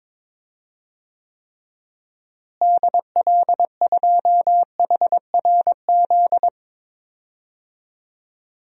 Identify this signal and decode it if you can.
Morse code, heard as a keyed tone, about 22 wpm, 700 Hz. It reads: DL2HRZ
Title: DXCC-100 Call Signs - 6 Characters - Repeated 3x - 22wpm